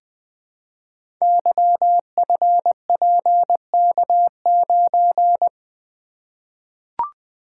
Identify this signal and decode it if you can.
Morse code, heard as a keyed tone, about 20 wpm, 700 Hz. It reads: YFPK9
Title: Code Groups - 5 Letter:Numbers - 20wpm